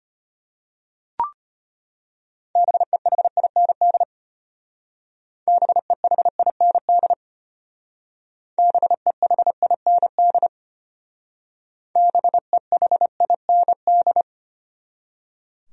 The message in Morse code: BEHIND BEHIND BEHIND BEHIND

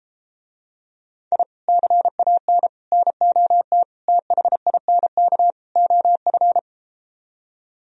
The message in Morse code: I CAN NOT THINK OF

